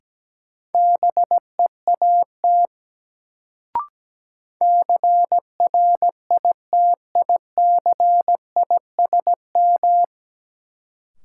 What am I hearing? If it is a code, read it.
Morse code, heard as a keyed tone, about 17 wpm, 700 Hz. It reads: BEAT CRITICISM
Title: Sentences from Top 1300 Words - New Word Drill - 17wpm